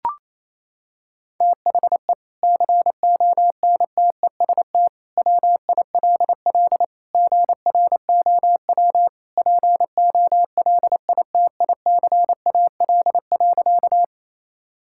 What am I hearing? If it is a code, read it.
Morse code, heard as a keyed tone, about 28 wpm, 700 Hz. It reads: THE CONTEST WILL GROW POLITICAL.
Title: Sentences from Top 1600 Words 28wpm